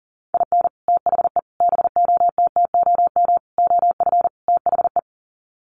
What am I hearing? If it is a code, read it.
Morse code, heard as a keyed tone, about 40 wpm, 700 Hz. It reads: IN THE BOTTOM OF THE
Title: Sets of 5 Words Encore 40wpm